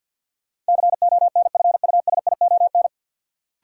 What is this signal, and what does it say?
Morse code, heard as a keyed tone, about 52 wpm, 700 Hz. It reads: CONFUSION
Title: Sentences from Top 1600 Words - New Word Drill Alt - 50wpm